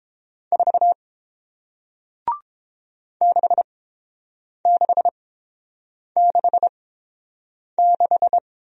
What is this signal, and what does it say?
Morse code, heard as a keyed tone, about 30 wpm, spaced as for 7 wpm, 700 Hz. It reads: 4 6666